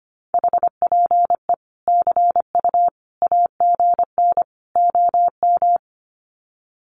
Morse code, heard as a keyed tone, about 25 wpm, 700 Hz. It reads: HPE CU AGN OM
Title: Common QSO Phrases 25wpm